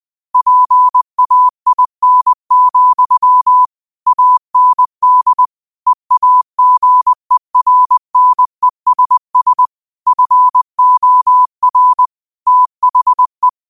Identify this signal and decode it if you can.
Morse code, heard as a keyed tone, about 20 wpm, 1000 Hz. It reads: PAIN, AND EAGERNESS FOR THE
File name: qotdfiles / 20wpm